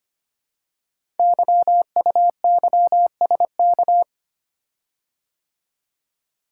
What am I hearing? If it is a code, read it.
Morse code, heard as a keyed tone, about 25 wpm, 700 Hz. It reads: YUYSK
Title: Code Groups - 5 Letters - 25wpm